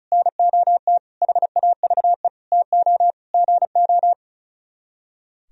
Morse code, heard as a keyed tone, about 45 wpm, 700 Hz. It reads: NOT HAVE TO GO